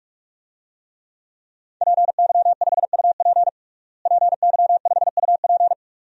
Morse code, heard as a keyed tone, about 46 wpm, 700 Hz. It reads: PY5UP PY5UP
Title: DXCC-100 Call Signs - 5 Characters - Repeated 3x - 45wpm